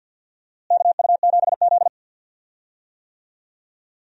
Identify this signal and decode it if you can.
Morse code, heard as a keyed tone, about 51 wpm, 700 Hz. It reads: KU7Z
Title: USA Call Signs - 4 Characters - Repeated 3x - 50wpm